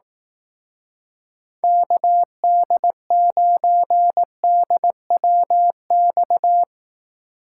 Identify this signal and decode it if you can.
Morse code, heard as a keyed tone, about 18 wpm, 700 Hz. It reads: KD9DWX
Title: Speed Racing - USA 2x3 Call Signs 15wpm